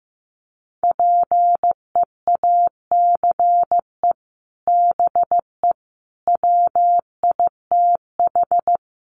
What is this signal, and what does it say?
Morse code, heard as a keyed tone, about 15 wpm, 700 Hz. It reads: PEACE BE WITH
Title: Sentences from Top 1500 Words 15wpm